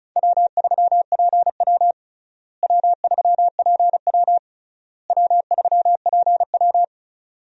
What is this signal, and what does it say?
Morse code, heard as a keyed tone, about 36 wpm, 700 Hz. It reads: W3PW W3PW W3PW
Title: USA Call Signs - 4 Characters - Repeated 3x - 35wpm